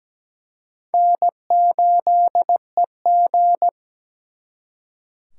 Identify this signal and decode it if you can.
Morse code, heard as a keyed tone, about 20 wpm, 700 Hz. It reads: N8EG
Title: Speed Racing - USA 1x2 Call Signs 17wpm